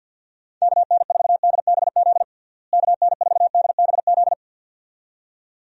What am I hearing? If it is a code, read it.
Morse code, heard as a keyed tone, about 53 wpm, 700 Hz. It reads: KN4DBZ KN4DBZ
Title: USA Call Signs - 6 Characters - Repeated 2x - 50wpm